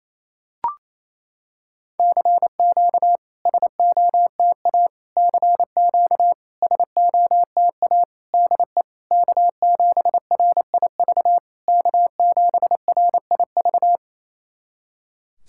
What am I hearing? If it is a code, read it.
Morse code, heard as a keyed tone, about 29 wpm, 690 Hz. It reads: CQ SOTA CQ SOTA DE K7RIV K7RIV
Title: SOTA QSO Practice 28wpm